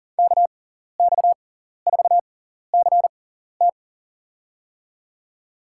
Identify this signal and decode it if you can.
Morse code, heard as a keyed tone, about 40 wpm, spaced as for 14 wpm, 700 Hz. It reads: KX4CT
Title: USA Call Signs - 6x Character Spacing - 40wpm